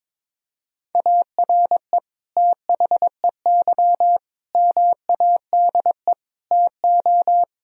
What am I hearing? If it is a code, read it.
Morse code, heard as a keyed tone, about 22 wpm, 700 Hz. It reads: ARE THEY MADE TO
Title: Sentences from Top 1000 Words - Review Part 2 - 22wpm